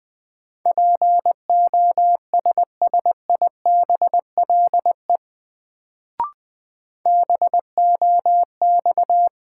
Morse code, heard as a keyed tone, about 20 wpm, 700 Hz. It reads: POSSIBLE BOX
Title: Sentences from Top 500 Words - New Word Drill - 20wpm